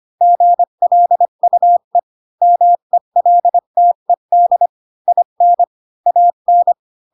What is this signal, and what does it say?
Morse code, heard as a keyed tone, about 25 wpm, 700 Hz. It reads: GLUE MELTED IN AN
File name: Pinocchio / Pinocchio ch8 20 25 ik1hge